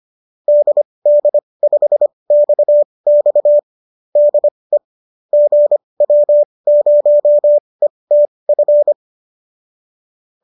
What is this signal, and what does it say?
Morse code, heard as a keyed tone, about 25 wpm, spaced as for 20 wpm, 590 Hz. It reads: DD5XX DE GW0ETF